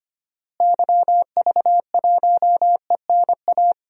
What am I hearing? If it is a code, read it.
Morse code, heard as a keyed tone, about 25 wpm, 700 Hz. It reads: YV1ENA
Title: DXCC-100 Call Signs - 6 Characters - Repeated 2x - 25wpm